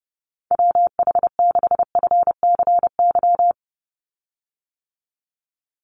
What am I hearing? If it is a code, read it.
Morse code, heard as a keyed tone, about 30 wpm, 700 Hz. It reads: WH6FCY
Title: USA Call Signs 30wpm